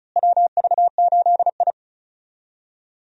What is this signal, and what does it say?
Morse code, heard as a keyed tone, about 34 wpm, 700 Hz. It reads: WV8I
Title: USA Call Signs - 4 Characters - Repeated 3x - 35wpm